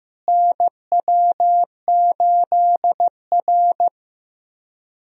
Morse code, heard as a keyed tone, about 15 wpm, 700 Hz. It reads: NW8R
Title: USA Call Signs - 4 Characters - Repeated 2x - 15wpm